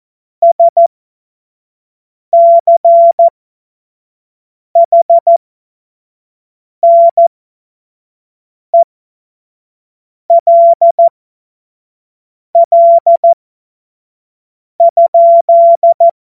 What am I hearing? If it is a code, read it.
Morse code, heard as a keyed tone, about 14 wpm, spaced as for 5 wpm, 680 Hz. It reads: SCHNELL?
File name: Morsekurs / Witze / Witz 09. November 2021, 25 F70